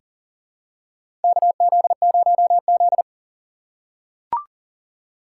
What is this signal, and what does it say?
Morse code, heard as a keyed tone, about 40 wpm, 700 Hz. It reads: KZ0Z
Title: USA Call Signs - 4 Characters - 3x Character Spacing - 40wpm